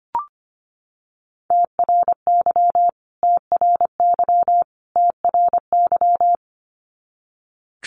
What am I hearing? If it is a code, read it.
Morse code, heard as a keyed tone, about 25 wpm, 700 Hz. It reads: TRY TRY TRY
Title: Top 500 Words Repeated 25wpm